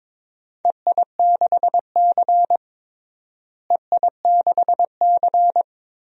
Morse code, heard as a keyed tone, about 22 wpm, 700 Hz. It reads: EI6C EI6C
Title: DXCC-100 Call Signs - 4 Characters - Repeated 2x - 22wpm